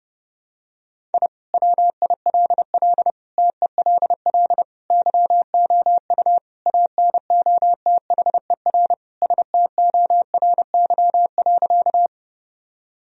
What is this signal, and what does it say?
Morse code, heard as a keyed tone, about 30 wpm, 700 Hz. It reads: I WILL TELL YOU ANOTHER STORY.
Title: Sentences from Top 1000 Words - Review Part 1 - 30wpm